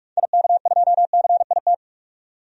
Morse code, heard as a keyed tone, about 44 wpm, 690 Hz. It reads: IK2CIT
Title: DXCC-100 Call Signs - 6 Characters - 45wpm